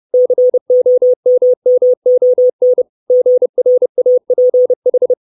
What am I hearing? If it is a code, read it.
Morse code, heard as a keyed tone, about 30 wpm, 500 Hz. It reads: COMMON GRAPH
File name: 632 Words 30WPM